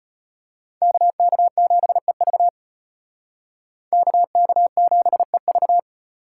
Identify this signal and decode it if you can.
Morse code, heard as a keyed tone, about 32 wpm, 700 Hz. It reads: KK7EV KK7EV